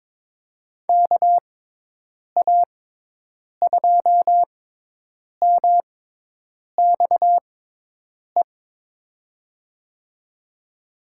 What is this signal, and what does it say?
Morse code, heard as a keyed tone, about 22 wpm, spaced as for 8 wpm, 700 Hz. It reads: KA2MXE
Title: USA Call Signs - 6x Character Spacing - 22wpm